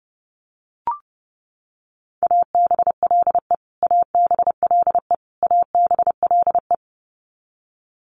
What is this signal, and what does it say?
Morse code, heard as a keyed tone, about 30 wpm, 700 Hz. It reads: ABLE ABLE ABLE
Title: Top 500 Words Repeated 30wpm